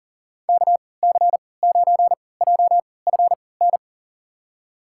Morse code, heard as a keyed tone, about 40 wpm, spaced as for 23 wpm, 700 Hz. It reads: KC9JFN
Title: USA Call Signs - 6 Characters - 3x Character Spacing - 40wpm